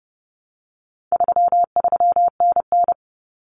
Morse code, heard as a keyed tone, about 30 wpm, 700 Hz. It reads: 33NN
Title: Common Cut Numbers - 4-Digit 30wpm